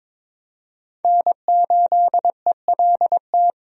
Morse code, heard as a keyed tone, about 22 wpm, 690 Hz. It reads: N8ELT
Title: USA Call Signs - 5 Characters - Repeated 3x - 22wpm